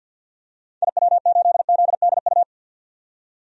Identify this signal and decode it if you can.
Morse code, heard as a keyed tone, about 52 wpm, 700 Hz. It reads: IW8ZDU
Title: DXCC-100 Call Signs - 6 Characters - 50wpm